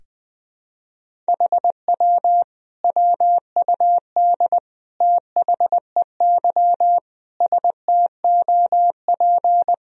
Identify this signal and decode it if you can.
Morse code, heard as a keyed tone, about 20 wpm, 700 Hz. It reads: HW WUD THEY STOP
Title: Sentences from Top 300 Words and Abbreviations 20wpm